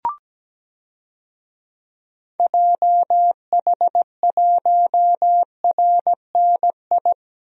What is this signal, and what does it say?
Morse code, heard as a keyed tone, about 17 wpm, 700 Hz. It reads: JH1RNI
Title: DXCC-100 Call Signs - 6 Characters - 17wpm